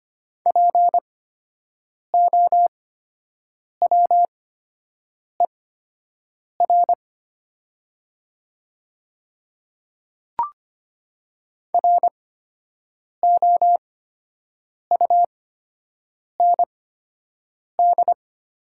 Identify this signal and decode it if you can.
Morse code, heard as a keyed tone, about 25 wpm, spaced as for 7 wpm, 700 Hz. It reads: POWER ROUND
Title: 8x Character Spacing - Top 500 Words - 25wpm